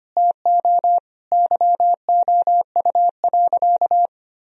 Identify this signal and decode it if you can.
Morse code, heard as a keyed tone, about 25 wpm, 700 Hz. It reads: TO YOU.
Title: Sentences from Top 1000 Words - Review Part 2 - 25wpm